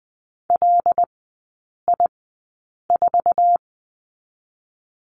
Morse code, heard as a keyed tone, about 20 wpm, spaced as for 8 wpm, 700 Hz. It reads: LI4
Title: 074 Lesson 13e - Letter L - Three Characters 20wpm